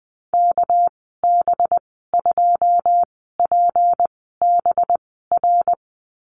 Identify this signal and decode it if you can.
Morse code, heard as a keyed tone, about 20 wpm, spaced as for 14 wpm, 700 Hz. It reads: KB2PBR
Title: 238 Lesson 38g - Letter q - Call Signs 20wpm